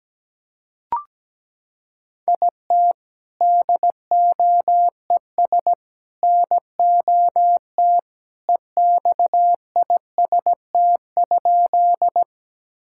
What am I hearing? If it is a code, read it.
Morse code, heard as a keyed tone, about 17 wpm, 700 Hz. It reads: IT DOES NOT EXIST?
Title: Sentences from Top 1000 Words - Review Part 5 - 17wpm